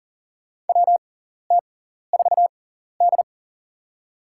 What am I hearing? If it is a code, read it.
Morse code, heard as a keyed tone, about 40 wpm, spaced as for 14 wpm, 700 Hz. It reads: WT4D
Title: USA Call Signs - 4 Characters - 6x Character Spacing - 40wpm